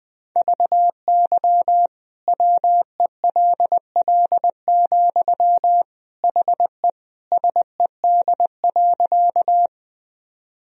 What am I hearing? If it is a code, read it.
Morse code, heard as a keyed tone, about 20 wpm, 700 Hz. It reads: VY WELL, HE SED.
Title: Sentences from Top 200 Words and Abbreviations 20wpm